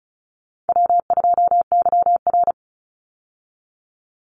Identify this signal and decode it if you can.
Morse code, heard as a keyed tone, about 34 wpm, 700 Hz. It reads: W2YR
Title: Call Signs Repeated 3x - 35wpm